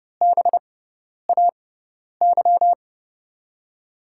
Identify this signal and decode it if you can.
Morse code, heard as a keyed tone, about 30 wpm, spaced as for 10 wpm, 700 Hz. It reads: BAY
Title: USA Call Signs - 5 Characters - 6x Character Spacing - 30wpm